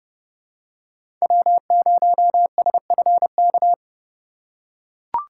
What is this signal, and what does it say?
Morse code, heard as a keyed tone, about 31 wpm, 700 Hz. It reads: W0SFK
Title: USA Call Signs - 4x Character Spacing - 30wpm